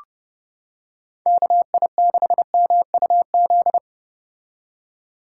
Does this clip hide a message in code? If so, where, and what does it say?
Yes: Morse code from 1.3 to 3.8 s, heard as a keyed tone, about 30 wpm, 700 Hz. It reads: KI6MUZ